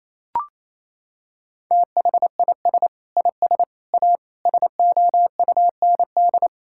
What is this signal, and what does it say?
Morse code, heard as a keyed tone, about 27 wpm, 700 Hz. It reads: THIS IS A SOUND
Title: Sentences from Top 1000 Words - Review Part 6 - 28wpm